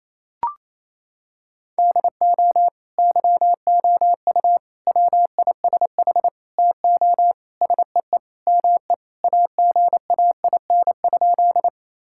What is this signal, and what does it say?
Morse code, heard as a keyed tone, about 28 wpm, 700 Hz. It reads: DO YOU WISH TO SEE ME AGAIN?
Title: Sentences from Top 1600 Words 28wpm